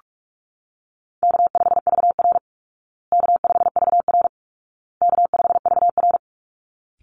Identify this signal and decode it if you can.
Morse code, heard as a keyed tone, about 44 wpm, 700 Hz. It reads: K5VR K5VR K5VR
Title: Easy Call Signs Repeated 3x - 45wpm